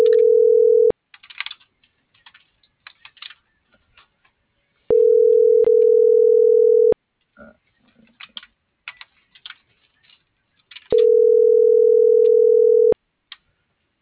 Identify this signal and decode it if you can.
Ringback tone (North America), 440 + 480 Hz.